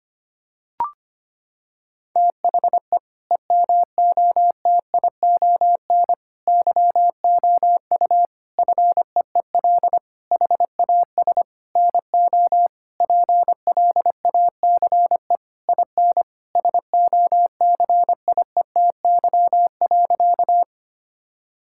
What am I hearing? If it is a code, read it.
Morse code, heard as a keyed tone, about 25 wpm, 700 Hz. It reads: THE EMOTION YOU FEEL HAS NO PLACE IN SOCIETY.
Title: Sentences from Top 1600 Words 25wpm